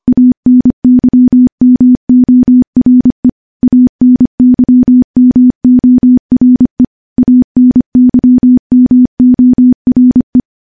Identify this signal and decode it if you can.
Morse code, heard as a keyed tone, about 25 wpm, 250 Hz. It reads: ANYMORE ANYMORE ANYMORE
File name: morse-training-25WPM